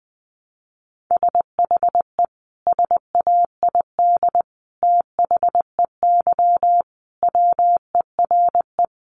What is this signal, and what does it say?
Morse code, heard as a keyed tone, about 20 wpm, 700 Hz. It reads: SHE SAID THEY WERE
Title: Sentences from Top 1100 Words 20wpm